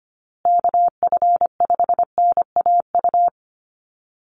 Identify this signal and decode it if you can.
Morse code, heard as a keyed tone, about 25 wpm, 700 Hz. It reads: KF5NAU
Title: USA Call Signs Repeated 2x 25wpm